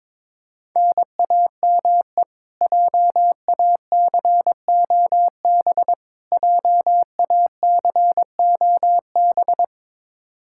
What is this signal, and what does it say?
Morse code, heard as a keyed tone, about 22 wpm, 700 Hz. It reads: NAME JACOB JACOB
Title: K1USN SST QSO Practice 22wpm